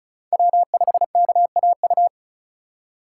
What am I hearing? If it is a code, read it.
Morse code, heard as a keyed tone, about 34 wpm, 700 Hz. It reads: W5KAU
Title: USA Call Signs - 6x Character Spacing - 35wpm